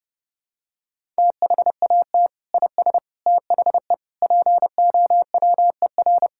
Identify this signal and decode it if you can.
Morse code, heard as a keyed tone, about 32 wpm, 700 Hz. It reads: THAT IS THE POWER
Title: Sentences from Top 1000 Words - Review Part 5 - 30wpm